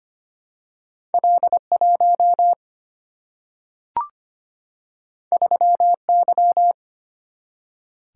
Morse code, heard as a keyed tone, about 25 wpm, 700 Hz. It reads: L1 3Y
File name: Two Letter-Number Combinations - 25wpm